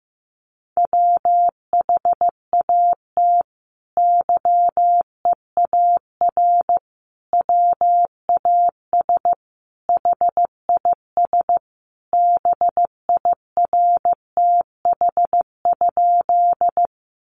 Morse code, heard as a keyed tone, about 15 wpm, 700 Hz. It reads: WHAT YEAR WAS HIS BIRTH?
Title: Sentences from Top 800 Words 15wpm